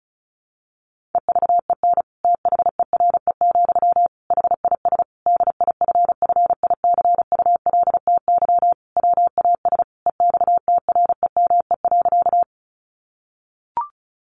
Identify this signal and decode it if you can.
Morse code, heard as a keyed tone, about 34 wpm, 700 Hz. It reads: EVEN THERE, HIS DIFFICULTY WAS EXTREME.
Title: Sentences from Top 1500 Words 35wpm